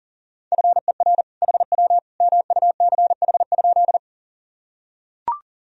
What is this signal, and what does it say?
Morse code, heard as a keyed tone, about 40 wpm, 700 Hz. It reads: FER HW MUCH?
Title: Sentences from Top 200 Words and Abbreviations 40wpm